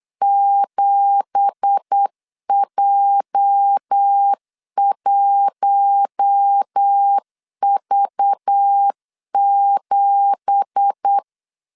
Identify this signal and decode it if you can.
Morse code, heard as a keyed tone, about 8 wpm, 800 Hz. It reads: 7J1V7
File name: 8wpm